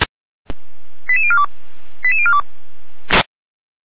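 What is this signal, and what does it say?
Five-tone selective call (ZVEI), heard as quick runs of five tones, tones of about 70 ms. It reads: 88042 88042
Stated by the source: Received on SDR, ZVEI five-tone selective call 89409.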